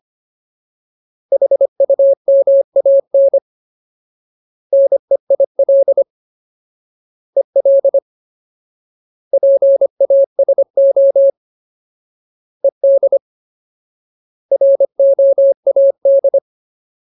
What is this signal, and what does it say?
Morse code, heard as a keyed tone, about 25 wpm, 550 Hz. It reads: HUMAN NEIL EL PASO ED ROAD
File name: Ultimate Word List / 25WPM Ultimate